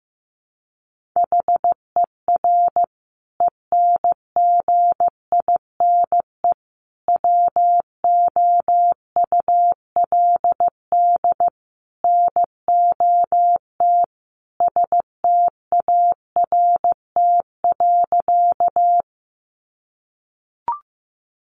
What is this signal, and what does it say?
Morse code, heard as a keyed tone, about 15 wpm, 700 Hz. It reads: HER ENGINE WOULD NOT START.
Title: Sentences from Top 1500 Words 15wpm